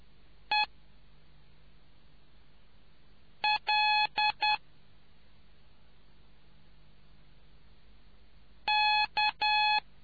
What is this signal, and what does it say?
Morse code, heard as a keyed tone, about 10 wpm, 850 Hz. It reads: E L K